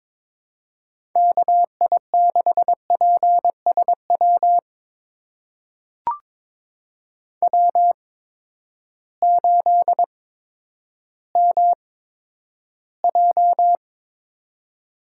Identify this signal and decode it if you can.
Morse code, heard as a keyed tone, about 23 wpm, spaced as for 11 wpm, 700 Hz. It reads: KI6PSW W8MJ
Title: USA Call Signs - 8x Character Spacing - 22wpm